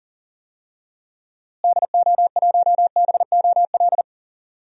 Morse code, heard as a keyed tone, about 40 wpm, 700 Hz. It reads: DO1BOL
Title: DXCC-100 Call Signs - 6 Characters - Repeated 2x - 40wpm